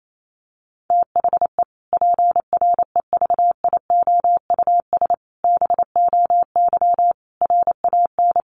Morse code, heard as a keyed tone, about 28 wpm, 700 Hz. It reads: THE PREVIOUS BOY RAN
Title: Sentences from Top 1500 Words 28wpm